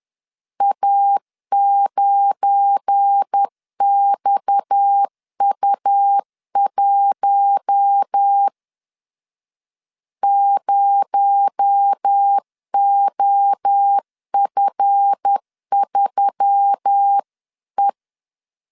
Morse code, heard as a keyed tone, about 10 wpm, 800 Hz. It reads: A9XU1 0OF3E
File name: CW / DET / 1 9 1500C S A 10wpm 4511